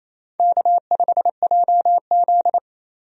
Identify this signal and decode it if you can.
Morse code, heard as a keyed tone, about 28 wpm, 700 Hz. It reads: K5JZ
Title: USA Call Signs - 4 Characters - Repeated 2x - 28wpm